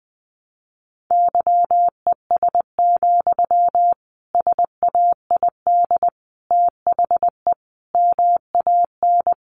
Morse code, heard as a keyed tone, about 20 wpm, 700 Hz. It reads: YES, SAID THE MAN
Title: A Sound Of Thunder 20wpm